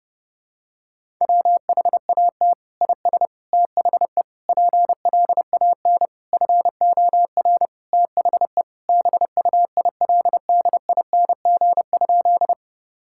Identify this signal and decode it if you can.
Morse code, heard as a keyed tone, about 28 wpm, 700 Hz. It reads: WHAT IS THE PLAN FOR THE BUILDING?
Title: Sentences from Top 1000 Words - Review Part 5 - 30wpm